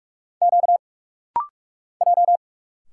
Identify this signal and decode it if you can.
Morse code, heard as a keyed tone, about 46 wpm, 700 Hz. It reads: Q J
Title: Single Letters - Rapid-Fire 45wpm